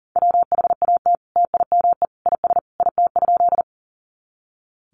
Morse code, heard as a keyed tone, about 40 wpm, 700 Hz. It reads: WHAT TIME IS IT?
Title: Sentences from Top 100 Words 40wpm